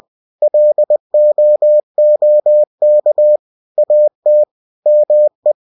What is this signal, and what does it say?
Morse code, heard as a keyed tone, about 20 wpm, 600 Hz. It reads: LOOK AT ME
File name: cw-books / winnie-the-pooh / winnie-20wpm-page-0087